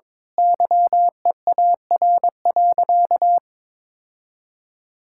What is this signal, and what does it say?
Morse code, heard as a keyed tone, about 22 wpm, 700 Hz. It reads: YEAR.
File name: practice / 22wpm / Sentences with 8-Word Phrases - 22wpm